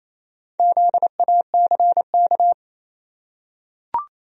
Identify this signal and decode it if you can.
Morse code, heard as a keyed tone, about 27 wpm, 700 Hz. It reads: ZACK
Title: USA Common Names - Max 4 Letters - 28wpm